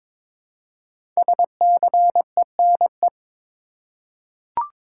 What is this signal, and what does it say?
Morse code, heard as a keyed tone, about 22 wpm, 700 Hz. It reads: SCENE
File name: 5 Letter Words - First 3 Letters Given - 22wpm